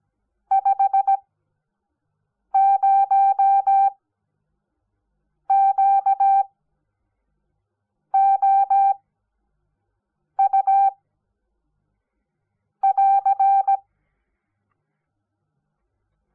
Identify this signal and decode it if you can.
Morse code, heard as a keyed tone, about 17 wpm, spaced as for 5 wpm, 780 Hz. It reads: 50QOU <AR>
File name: Onlinelehrgang / morsen / DJ4UF Morsen05